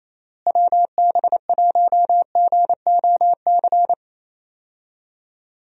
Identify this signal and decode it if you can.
Morse code, heard as a keyed tone, about 28 wpm, 700 Hz. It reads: WB1GOC
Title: USA Call Signs - 6 Characters - Repeated 3x - 28wpm